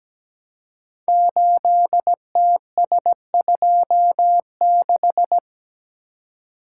Morse code, heard as a keyed tone, about 17 wpm, 700 Hz. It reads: 8TS26
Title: Code Groups - 5 Letter:Numbers - 17wpm